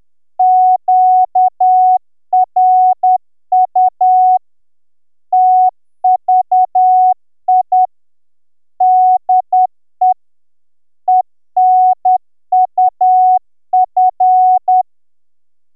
Morse code, heard as a keyed tone, about 10 wpm, 740 Hz. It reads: QRU TVI DE ENUF